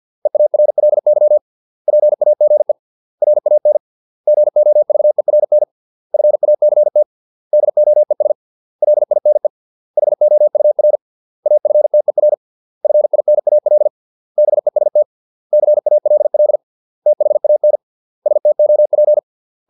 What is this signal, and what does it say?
Morse code, heard as a keyed tone, about 50 wpm, 600 Hz. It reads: EARLY PAGE RAN GOVERN FACT DOES LINE HOUR AFTER FINAL BEST CALL THAN STOP